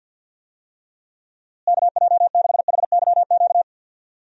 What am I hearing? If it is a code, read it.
Morse code, heard as a keyed tone, about 46 wpm, 700 Hz. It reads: KJ6HYQ
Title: USA Call Signs - 6 Characters - Repeated 2x - 50wpm